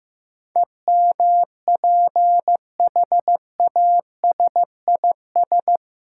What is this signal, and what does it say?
Morse code, heard as a keyed tone, about 15 wpm, 700 Hz. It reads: EMPHASIS
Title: Sentences from Top 1400 Words - New Word Drill - 15wpm